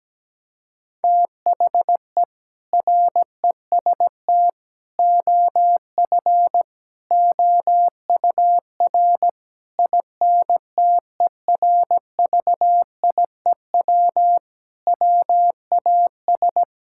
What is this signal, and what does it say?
Morse code, heard as a keyed tone, about 17 wpm, 700 Hz. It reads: THE REST OF OUR INTERVIEW WAS
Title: Sentences from Top 1600 Words 17wpm